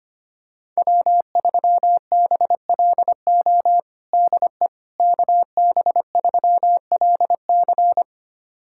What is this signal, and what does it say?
Morse code, heard as a keyed tone, about 25 wpm, 700 Hz. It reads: W3BLO DE KB3LC
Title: USA Callsign DE Callsign - 5 Characters - 25wpm